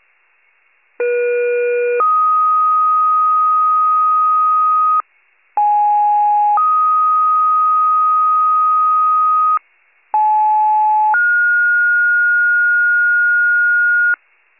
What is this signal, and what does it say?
Two-tone page, heard as a one-second tone, then a three-second tone, about 480 then 1250 Hz, about 830 then 1250 Hz, about 860 then 1430 Hz. It is heard from 1.0 to 14.2 s.